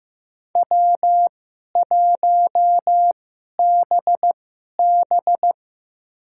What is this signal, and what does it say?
Morse code, heard as a keyed tone, about 15 wpm, spaced as for 11 wpm, 700 Hz. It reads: W1BB
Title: USA Call Signs - 4 Characters - 2x Character Spacing - 15wpm